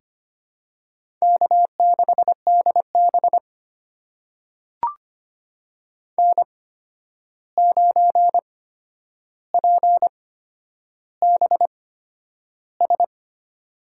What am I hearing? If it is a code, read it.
Morse code, heard as a keyed tone, about 25 wpm, spaced as for 10 wpm, 700 Hz. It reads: K6DB N9PBS
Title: USA Call Signs - 8x Character Spacing - 25wpm